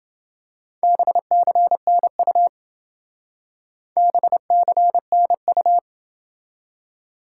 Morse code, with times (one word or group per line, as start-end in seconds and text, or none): 0.8-2.5 BCNU
4.0-5.8 BCNU